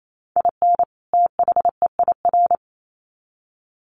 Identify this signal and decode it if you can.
Morse code, heard as a keyed tone, about 28 wpm, 700 Hz. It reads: IN THEIR
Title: Sets of 2 Words 28wpm